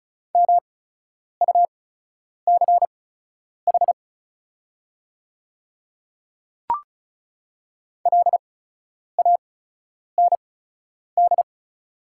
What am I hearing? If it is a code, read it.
Morse code, heard as a keyed tone, about 36 wpm, spaced as for 10 wpm, 690 Hz. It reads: MUCH LAND
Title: 8x Character Spacing - Top 200 Words - 35wpm